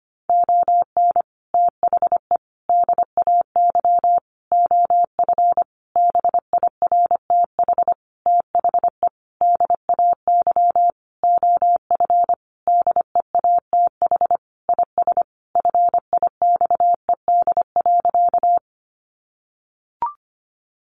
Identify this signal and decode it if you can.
Morse code, heard as a keyed tone, about 25 wpm, 700 Hz. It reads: ON THE DAY OF BIRTH THE DAY OF DEATH IS FIXED.
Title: Sentences from Top 1100 Words 25wpm